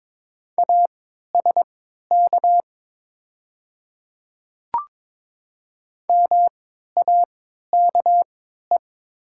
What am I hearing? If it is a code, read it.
Morse code, heard as a keyed tone, about 22 wpm, spaced as for 13 wpm, 700 Hz. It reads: ASK MAKE
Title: 3x Character Spacing - Top 300 Words - 22wpm